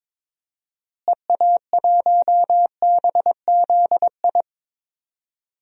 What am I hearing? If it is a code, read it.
Morse code, heard as a keyed tone, about 22 wpm, 700 Hz. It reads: EA1BZI
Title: DXCC-100 Call Signs - 6 Characters - Repeated 3x - 22wpm